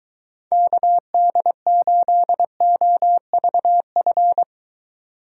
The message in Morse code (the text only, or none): KD8OVF